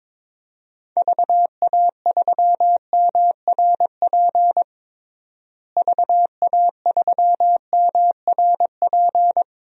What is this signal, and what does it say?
Morse code, heard as a keyed tone, about 22 wpm, 700 Hz. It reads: VA3MRP VA3MRP
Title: DXCC-100 Call Signs - 6 Characters - Repeated 2x - 22wpm